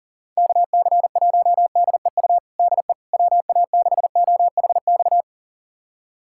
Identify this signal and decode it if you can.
Morse code, heard as a keyed tone, about 40 wpm, 700 Hz. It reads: KC1DEU DE WA6OHX
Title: USA Callsign DE Callsign - 6 Characters - 40wpm